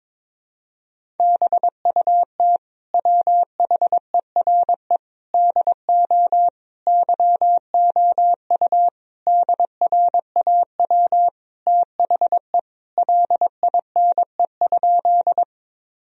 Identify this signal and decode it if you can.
Morse code, heard as a keyed tone, about 22 wpm, 700 Hz. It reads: BUT WHERE DO YOU DRAW THE LINE?
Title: Sentences from Top 1000 Words - Review Part 1 - 22wpm